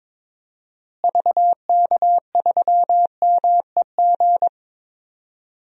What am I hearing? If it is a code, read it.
Morse code, heard as a keyed tone, about 22 wpm, 700 Hz. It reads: VK3MEG
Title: DXCC-100 Call Signs - 6 Characters - Repeated 3x - 22wpm